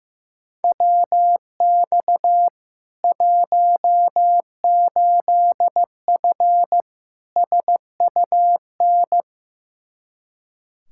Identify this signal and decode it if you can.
Morse code, heard as a keyed tone, about 15 wpm, 700 Hz. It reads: WX 18F SUN
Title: Practice QSO - Exchange 2 - Info - 15wpm